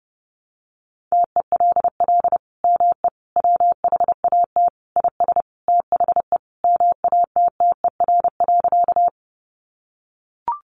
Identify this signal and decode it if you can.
Morse code, heard as a keyed tone, about 30 wpm, 700 Hz. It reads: TELL ME WHAT IS THE MATTER.
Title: Sentences from Top 700 Words 30wpm